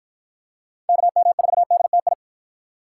Morse code, heard as a keyed tone, about 53 wpm, 700 Hz. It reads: KM4DTI